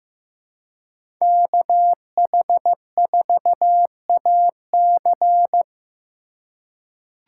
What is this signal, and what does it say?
Morse code, heard as a keyed tone, about 15 wpm, 700 Hz. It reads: KH4AC
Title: USA Call Signs - 2x Character Spacing - 15wpm